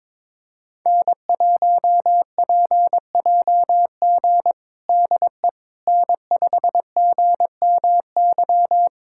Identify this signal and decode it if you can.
Morse code, heard as a keyed tone, about 22 wpm, 700 Hz. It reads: N1PJG DE N5GMY
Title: USA Callsign DE Callsign - 5 Characters - 22wpm